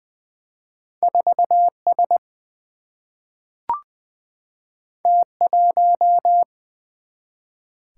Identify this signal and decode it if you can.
Morse code, heard as a keyed tone, about 20 wpm, 700 Hz. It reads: 4 S T 1
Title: Single-Letter Single-Number 20wpm